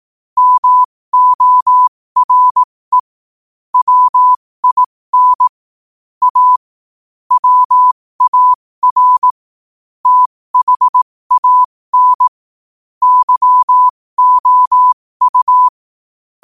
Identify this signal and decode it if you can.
Morse code, heard as a keyed tone, about 18 wpm, spaced as for 15 wpm, 1000 Hz. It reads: MORE WIN A WAR THAN YOU
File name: qotdfiles / 15wpm